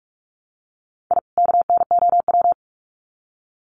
Morse code, heard as a keyed tone, about 44 wpm, 700 Hz. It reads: I KNOW
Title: Sets of 2 Words 45wpm